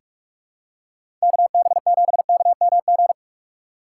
Morse code, heard as a keyed tone, about 42 wpm, 700 Hz. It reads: KB7KMG
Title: USA Call Signs - 6 Characters - 4x Character Spacing - 45wpm